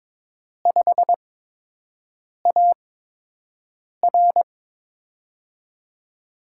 Morse code, heard as a keyed tone, about 22 wpm, spaced as for 6 wpm, 700 Hz. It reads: 5AR
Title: USA Call Signs - 5 Characters - 8x Character Spacing - 22wpm